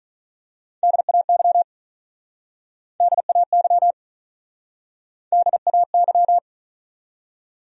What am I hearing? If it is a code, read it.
Morse code, heard as a keyed tone, about 43 wpm, 700 Hz. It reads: DAY DAY DAY
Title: Speed-Racing Top 300 Words 35wpm